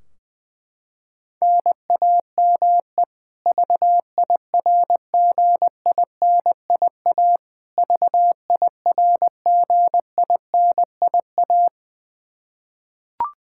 Morse code, heard as a keyed tone, about 20 wpm, 700 Hz. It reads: NAME VIRGINIA VIRGINIA
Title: Practice QSO - Exchange 1 - Beginning - 20wpm